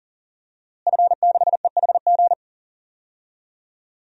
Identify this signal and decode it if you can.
Morse code, heard as a keyed tone, about 40 wpm, 700 Hz. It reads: F6EHG
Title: DXCC-100 Call Signs - 5 Characters - Repeated 3x - 40wpm